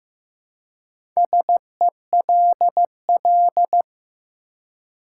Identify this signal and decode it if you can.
Morse code, heard as a keyed tone, about 15 wpm, 700 Hz. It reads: SELL
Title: Four Letter Words - 15wpm